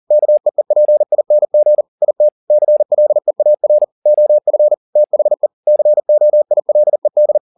Morse code, heard as a keyed tone, about 40 wpm, 600 Hz. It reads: KEEPING IT CLEAR OF THE COILED